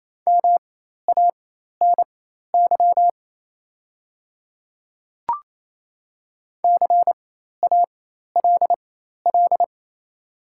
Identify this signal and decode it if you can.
Morse code, heard as a keyed tone, about 28 wpm, spaced as for 13 wpm, 690 Hz. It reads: MANY CALL
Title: 4x Character Spacing - Top 100 Words - 28wpm